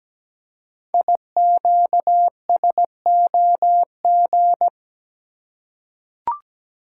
Morse code, heard as a keyed tone, about 17 wpm, 700 Hz. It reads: IQSOG
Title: Code Groups - 5 Letters - 17wpm